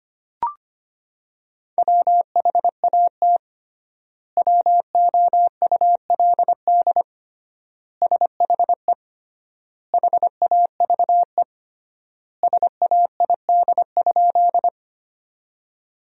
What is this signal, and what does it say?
Morse code, heard as a keyed tone, about 25 wpm, 700 Hz. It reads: WHAT WOULD SHE HAVE SAID?
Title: 3x Word-Spacing - Sentences from Top 100 Words - 25wpm